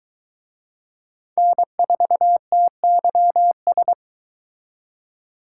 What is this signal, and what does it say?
Morse code, heard as a keyed tone, about 22 wpm, 700 Hz. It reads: N4TYS